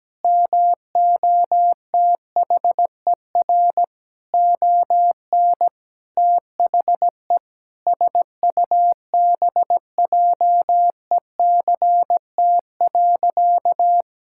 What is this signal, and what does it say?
Morse code, heard as a keyed tone, about 17 wpm, 700 Hz. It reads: MOTHER ON THE SUBJECT.
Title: Sentences from Top 1000 Words - Review Part 6 - 17wpm